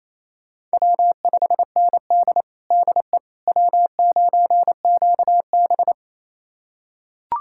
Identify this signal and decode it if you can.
Morse code, heard as a keyed tone, about 28 wpm, 700 Hz. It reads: W5ND DE W9QB
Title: USA Callsign DE Callsign - 4 Characters - 28wpm